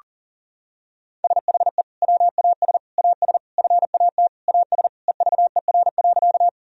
Morse code, heard as a keyed tone, about 40 wpm, 690 Hz. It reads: SHE WAS AS FAT AS EVER.
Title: Sentences from Top 1000 Words - Review Part 4 - 40wpm